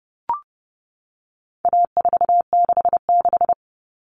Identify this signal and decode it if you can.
Morse code, heard as a keyed tone, about 30 wpm, 700 Hz. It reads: A466
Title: Common Cut Numbers - 4-Digit 30wpm